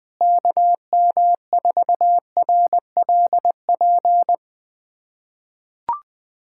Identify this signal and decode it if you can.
Morse code, heard as a keyed tone, about 21 wpm, 700 Hz. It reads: KM4RLP
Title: USA Callsign DE Callsign - 6 Characters - 20wpm